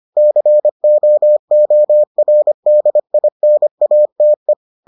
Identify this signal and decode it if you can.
Morse code, heard as a keyed tone, about 25 wpm, 600 Hz. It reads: COORDINATE